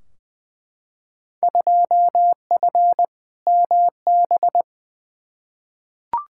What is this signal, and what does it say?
Morse code, heard as a keyed tone, about 20 wpm, 700 Hz. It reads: 2F MB
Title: Field Day Exchanges Encore - 20wpm